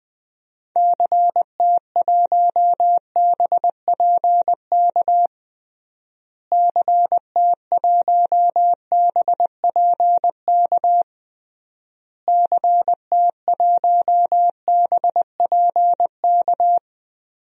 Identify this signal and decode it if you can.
Morse code, heard as a keyed tone, about 20 wpm, 700 Hz. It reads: CT1BPK CT1BPK CT1BPK
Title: DXCC-100 Call Signs - 6 Characters - Repeated 3x - 20wpm